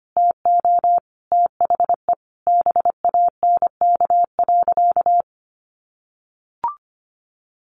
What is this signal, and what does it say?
Morse code, heard as a keyed tone, about 25 wpm, 700 Hz. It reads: TO THE BANK.
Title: Sentences from Top 1500 Words 25wpm